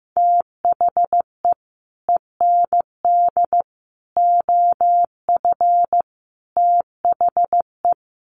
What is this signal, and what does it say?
Morse code, heard as a keyed tone, about 15 wpm, 700 Hz. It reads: THE END OF THE
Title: Sets of 5 Words Encore 15wpm